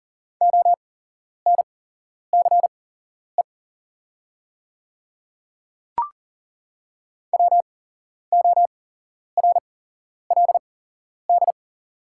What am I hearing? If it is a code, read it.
Morse code, heard as a keyed tone, about 40 wpm, spaced as for 11 wpm, 690 Hz. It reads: ONCE WORLD
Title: 8x Character Spacing - Top 300 Words - 40wpm